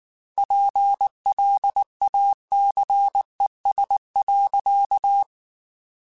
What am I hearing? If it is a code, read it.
Morse code, heard as a keyed tone, about 19 wpm, 800 Hz. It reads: PLACES.
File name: Chuckles / Chuckles-mp3s / MiddleAge @18